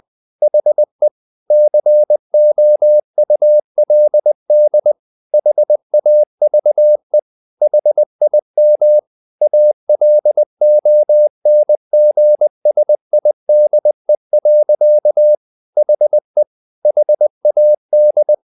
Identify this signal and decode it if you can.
Morse code, heard as a keyed tone, about 20 wpm, 600 Hz. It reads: HE COULD HAVE HIM ALONGSIDE. HE HAD
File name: the old man and the sea part 2 20wpm